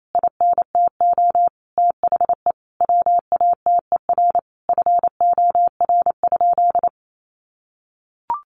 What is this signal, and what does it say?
Morse code, heard as a keyed tone, about 28 wpm, 700 Hz. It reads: INTO THE WATER FOR?
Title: Sentences from Top 1300 Words 28wpm